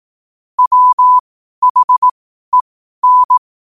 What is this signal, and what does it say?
Morse code, heard as a keyed tone, about 18 wpm, spaced as for 12 wpm, 1000 Hz. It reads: WHEN
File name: qotdfiles / 12wpm